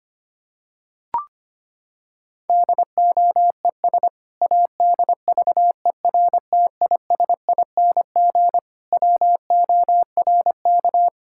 Morse code, heard as a keyed tone, about 25 wpm, 700 Hz. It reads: DOES ADVERTISING WORK
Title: Sentences from Top 1000 Words - Review Part 6 - 25wpm